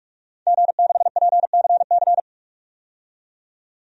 Morse code, heard as a keyed tone, about 42 wpm, 700 Hz. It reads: G6PCC
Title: DXCC-100 Call Signs - 5 Characters - 45wpm